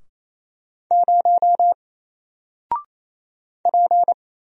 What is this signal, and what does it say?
Morse code, heard as a keyed tone, about 27 wpm, 700 Hz. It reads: 0 P
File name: Single-Letter-Number-28wpm